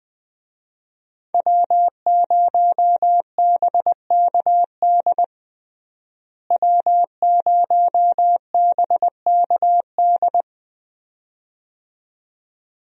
Morse code, heard as a keyed tone, about 20 wpm, 700 Hz. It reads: W0BKD W0BKD
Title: USA Call Signs - 5 Characters - Repeated 2x - 20wpm